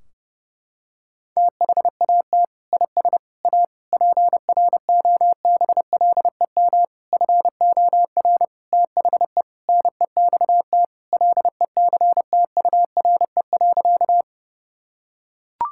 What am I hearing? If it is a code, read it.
Morse code, heard as a keyed tone, about 30 wpm, 700 Hz. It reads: THAT IS A PROBLEM FOR THE NEXT LECTURE.
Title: Sentences from Top 1600 Words 30wpm